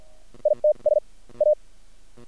Morse code, heard as a keyed tone, about 50 wpm, spaced as for 39 wpm, 630 Hz. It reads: ITS A
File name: archived / CRSnet